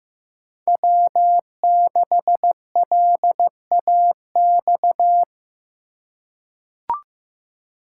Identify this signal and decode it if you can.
Morse code, heard as a keyed tone, about 15 wpm, 700 Hz. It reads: W6LAX
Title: K1USN SST QSO Practice - 15wpm